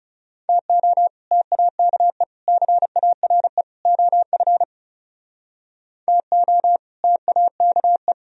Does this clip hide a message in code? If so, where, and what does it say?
Yes: Morse code from 0.5 to 8.1 s, heard as a keyed tone, about 39 wpm, 700 Hz. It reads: TO TAKE CARE OF TO TAKE